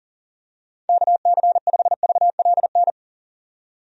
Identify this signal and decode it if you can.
Morse code, heard as a keyed tone, about 40 wpm, 700 Hz. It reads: KC5VLN
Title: USA Call Signs - 6 Characters - 40wpm